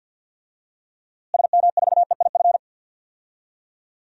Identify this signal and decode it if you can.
Morse code, heard as a keyed tone, about 53 wpm, 700 Hz. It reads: SM4EIF
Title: DXCC-100 Call Signs - 6 Characters - Repeated 3x - 50wpm